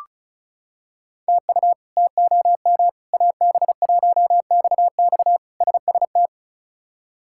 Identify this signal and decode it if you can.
Morse code, heard as a keyed tone, about 36 wpm, 700 Hz. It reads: TU TOM AB1XX SST
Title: K1USN SST QSO Practice - 35wpm